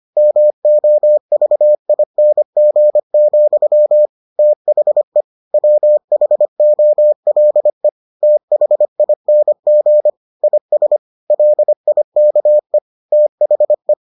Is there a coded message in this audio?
Yes: Morse code from 0.2 to 13.9 s, heard as a keyed tone, about 25 wpm, 600 Hz. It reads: MOVING, THE WHOLE THING IS LIKE THE